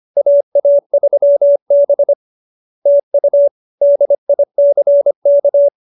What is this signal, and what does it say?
Morse code, heard as a keyed tone, about 25 wpm, 570 Hz. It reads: AA3B TU DICK
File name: QSO302 25